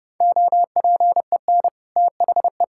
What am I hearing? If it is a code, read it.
Morse code, heard as a keyed tone, about 32 wpm, 700 Hz. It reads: OPEN THE
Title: Sentences from Top 1600 Words 30wpm